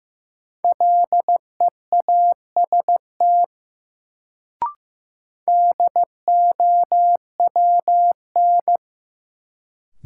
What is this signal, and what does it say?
Morse code, heard as a keyed tone, about 15 wpm, 700 Hz. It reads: LEAST DOWN
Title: Sets of 2 Words - New Word Drill - 15wpm